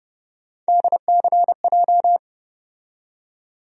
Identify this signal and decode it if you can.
Morse code, heard as a keyed tone, about 30 wpm, 700 Hz. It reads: DCJ
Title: USA Call Sign Postfixes - 3 Letters 30wpm